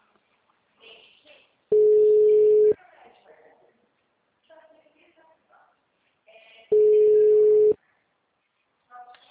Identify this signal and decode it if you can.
Ringback tone (Europe, 425 Hz), with speech.